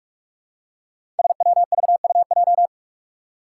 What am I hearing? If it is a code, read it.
Morse code, heard as a keyed tone, about 44 wpm, 700 Hz. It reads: SWVUJ
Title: Five Challenging Letter-Number Combinations 45wpm